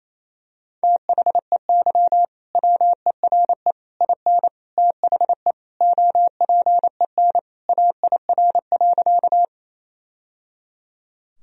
Morse code, heard as a keyed tone, about 28 wpm, 700 Hz. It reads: THEY WERE IN THE OPEN AIR.
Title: Sentences from Top 1000 Words - Review Part 1 - 28wpm